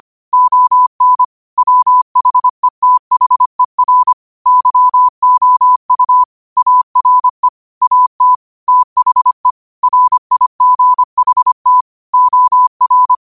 Morse code, heard as a keyed tone, about 25 wpm, 1000 Hz. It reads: ON WHETHER YOU ARE AT THE RIGHT OR
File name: 25wpm